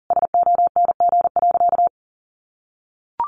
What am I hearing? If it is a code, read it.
Morse code, heard as a keyed tone, about 40 wpm, 700 Hz. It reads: SONG.
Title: Sentences from Top 400 Words 40wpm